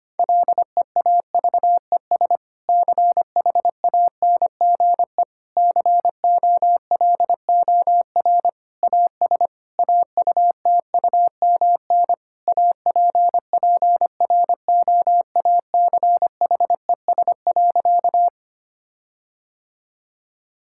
Morse code, heard as a keyed tone, about 25 wpm, 700 Hz. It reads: LEAVES CHANGE COLOR AS AUTUMN APPROACHES.
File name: practice / 25wpm / Difficult Sentences with Hints - 25wpm